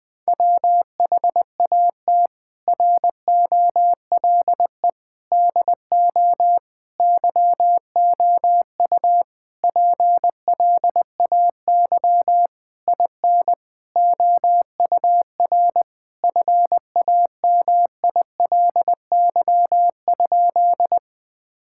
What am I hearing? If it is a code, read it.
Morse code, heard as a keyed tone, about 21 wpm, 700 Hz. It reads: WHAT ROLE DO YOU PLAY IN OUR FAMILY?
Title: Sentences from Top 1000 Words - Review Part 6 - 20wpm